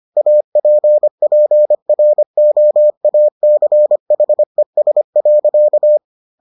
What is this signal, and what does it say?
Morse code, heard as a keyed tone, about 25 wpm, 600 Hz. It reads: APPROACHES.